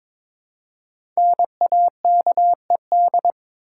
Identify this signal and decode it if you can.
Morse code, heard as a keyed tone, about 22 wpm, 690 Hz. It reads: NAKED
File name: 5 Letter Words - First 3 Letters Given - 22wpm